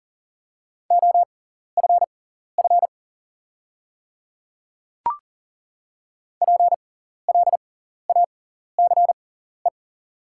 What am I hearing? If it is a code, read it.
Morse code, heard as a keyed tone, about 40 wpm, spaced as for 14 wpm, 690 Hz. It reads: OFF PLACE
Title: 6x Character Spacing - Top 300 Words - 40wpm